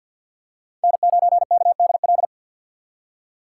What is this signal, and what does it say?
Morse code, heard as a keyed tone, about 48 wpm, 690 Hz. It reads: N9KDL
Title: USA Call Signs - 2x Character Spacing - 50wpm